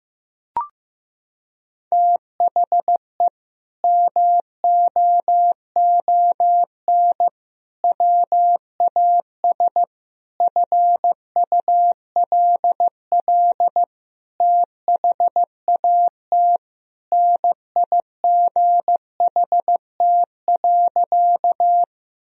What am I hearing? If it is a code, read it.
Morse code, heard as a keyed tone, about 16 wpm, 700 Hz. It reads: THE MOON WAS FULL THAT NIGHT.
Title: Sentences from Top 1000 Words - Review Part 3 - 15wpm